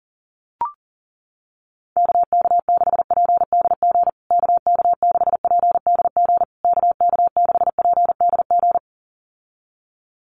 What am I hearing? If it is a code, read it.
Morse code, heard as a keyed tone, about 37 wpm, 700 Hz. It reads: KK6PDG KK6PDG KK6PDG
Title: USA Call Signs Repeated 3x 40wpm